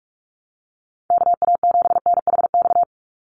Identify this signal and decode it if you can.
Morse code, heard as a keyed tone, about 44 wpm, 700 Hz. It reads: KA7NHX
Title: USA Call Signs 45wpm